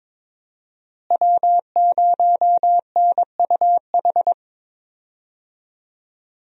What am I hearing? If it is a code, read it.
Morse code, heard as a keyed tone, about 21 wpm, 700 Hz. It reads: W0NUH
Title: USA Call Signs - 5 Characters - Repeated 2x - 22wpm